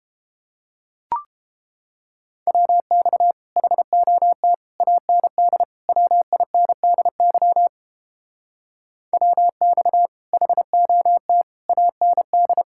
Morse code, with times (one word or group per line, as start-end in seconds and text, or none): 2.5-3.3 WX
3.6-4.5 HOT
4.8-5.6 AND
5.9-7.7 WINDY
9.1-10.1 WX
10.3-11.4 HOT
11.7-12.6 AND